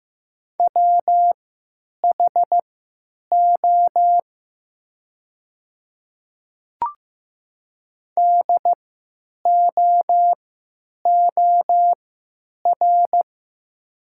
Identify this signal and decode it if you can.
Morse code, heard as a keyed tone, about 15 wpm, spaced as for 9 wpm, 700 Hz. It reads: WHO DOOR
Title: 3x Character Spacing - Top 300 Words - 15wpm